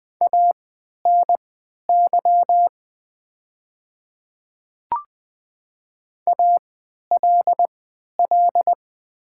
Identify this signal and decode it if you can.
Morse code, heard as a keyed tone, about 20 wpm, spaced as for 11 wpm, 700 Hz. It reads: ANY ALL
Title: 3x Character Spacing - Top 200 Words - 20wpm